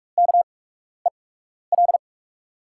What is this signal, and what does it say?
Morse code, heard as a keyed tone, about 47 wpm, spaced as for 12 wpm, 700 Hz. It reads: KEL
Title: USA Call Signs - 6 Characters - 8x Character Spacing - 45wpm